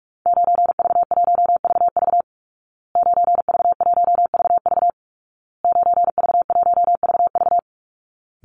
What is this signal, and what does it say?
Morse code, heard as a keyed tone, about 44 wpm, 700 Hz. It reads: 9V1VV 9V1VV 9V1VV
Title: Call Signs Repeated 2x - 45wpm